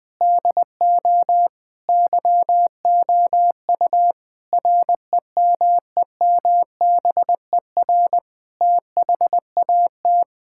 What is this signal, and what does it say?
Morse code, heard as a keyed tone, about 21 wpm, 700 Hz. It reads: DO YOU REMEMBER THAT
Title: Sentences from Top 1000 Words - Review Part 5 - 20wpm